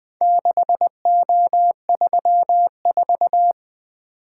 Morse code, heard as a keyed tone, about 20 wpm, 700 Hz. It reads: 6O34
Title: Four Letter-Number Combinations - 20wpm